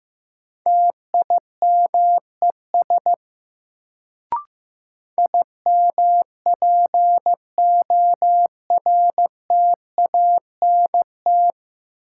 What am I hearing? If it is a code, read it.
Morse code, heard as a keyed tone, about 15 wpm, 700 Hz. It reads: TIMES IMPORTANT
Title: Sets of 3 Words - New Word Drill - 15wpm